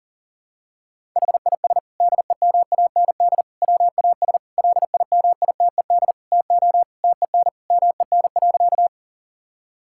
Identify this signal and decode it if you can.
Morse code, heard as a keyed tone, about 40 wpm, 700 Hz. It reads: HIS DEMAND WAS LIMITED TO TEN MEN.
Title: Sentences from Top 1000 Words - Review Part 6 - 40wpm